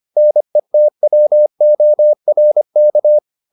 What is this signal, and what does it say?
Morse code, heard as a keyed tone, about 25 wpm, 600 Hz. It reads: NETWORK